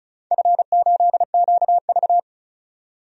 Morse code, heard as a keyed tone, about 34 wpm, 700 Hz. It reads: F8QV
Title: DXCC-100 Call Signs - 4 Characters - 35wpm